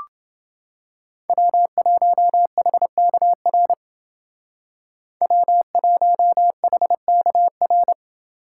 Morse code, heard as a keyed tone, about 28 wpm, 700 Hz. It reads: W1HKR W1HKR